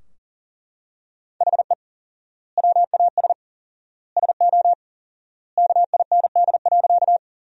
Morse code, heard as a keyed tone, about 40 wpm, 700 Hz. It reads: HE WAS SO KIND.
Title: 4x Word-Spacing - Sentences from Top 200 Words - 40wpm